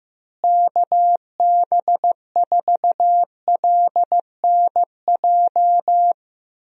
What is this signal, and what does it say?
Morse code, heard as a keyed tone, about 15 wpm, 700 Hz. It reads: KB4LNJ
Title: USA Call Signs - 6 Characters - Repeated 3x - 15wpm